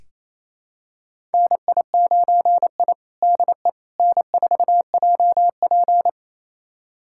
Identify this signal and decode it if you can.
Morse code, heard as a keyed tone, about 28 wpm, 700 Hz. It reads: NI9I DE N4JP
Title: USA Callsign DE Callsign - 4 Characters - 28wpm